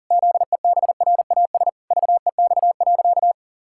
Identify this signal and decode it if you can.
Morse code, heard as a keyed tone, about 40 wpm, 700 Hz. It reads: ZEBRAS VEX.
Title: Pangram Challenge v2 - 40wpm